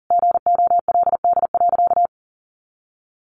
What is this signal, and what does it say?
Morse code, heard as a keyed tone, about 40 wpm, 700 Hz. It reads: GOLD.
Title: Sentences from Top 1200 Words 40wpm